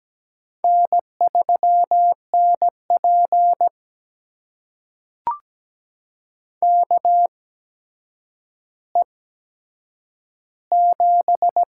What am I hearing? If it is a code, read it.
Morse code, heard as a keyed tone, about 17 wpm, spaced as for 8 wpm, 700 Hz. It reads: N3NP KE7
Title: USA Call Signs - 4 Characters - 8x Character Spacing - 17wpm